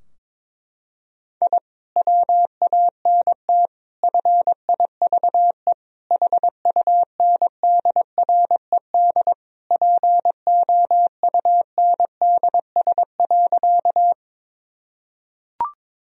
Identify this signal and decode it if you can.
Morse code, heard as a keyed tone, about 21 wpm, 700 Hz. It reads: I WANT FIVE HUNDRED POUNDS.
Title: Sentences from Top 1000 Words - Review Part 2 - 22wpm